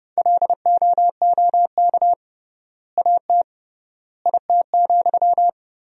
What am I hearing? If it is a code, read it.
Morse code, heard as a keyed tone, about 30 wpm, 700 Hz. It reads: LOOK AT IT,
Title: 3x Word-Spacing - Sentences from Top 200 Words - 30wpm